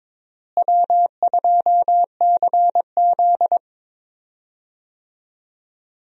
Morse code, heard as a keyed tone, about 22 wpm, 700 Hz. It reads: W2CZ
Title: USA Call Signs - 4 Characters - Repeated 2x - 22wpm